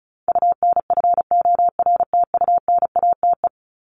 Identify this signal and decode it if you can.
Morse code, heard as a keyed tone, about 34 wpm, 700 Hz. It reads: UNFORTUNATE
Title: LY Suffix - Top 100 Words with Base Words 35wpm